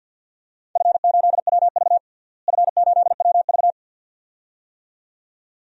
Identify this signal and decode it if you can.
Morse code, heard as a keyed tone, about 52 wpm, 700 Hz. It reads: F8WV F8WV
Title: DXCC-100 Call Signs - 4 Characters - Repeated 2x - 50wpm